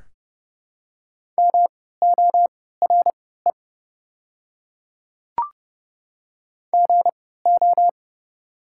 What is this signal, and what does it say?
Morse code, heard as a keyed tone, about 30 wpm, spaced as for 17 wpm, 690 Hz. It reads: MORE GO
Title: 3x Character Spacing - Top 100 Words - 30wpm